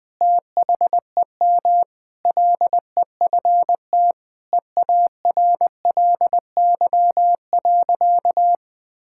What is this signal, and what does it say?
Morse code, heard as a keyed tone, about 20 wpm, 700 Hz. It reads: THEM LEFT EARLY.
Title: Sentences with 3-Word Phrases - Spoken Beginning - 20wpm